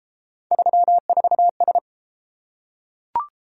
Morse code, heard as a keyed tone, about 31 wpm, 700 Hz. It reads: 34S